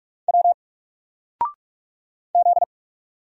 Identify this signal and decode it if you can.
Morse code, heard as a keyed tone, about 46 wpm, 700 Hz. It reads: W Z
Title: Single Letter-Number 45wpm